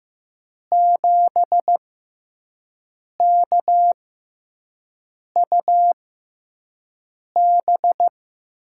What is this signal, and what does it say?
Morse code, heard as a keyed tone, about 15 wpm, spaced as for 5 wpm, 700 Hz. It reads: 7KUB
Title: USA Call Signs - 6x Character Spacing - 15wpm